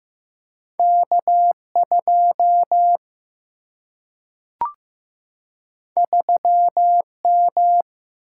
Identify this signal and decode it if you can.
Morse code, heard as a keyed tone, about 15 wpm, 700 Hz. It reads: K 2 3 M
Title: Single-Letter Single-Number 15wpm